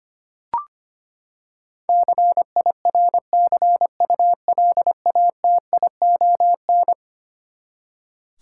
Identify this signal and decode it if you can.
Morse code, heard as a keyed tone, about 25 wpm, 700 Hz. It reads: CIRCULATION
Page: morse-code-ninja-practice-25wpm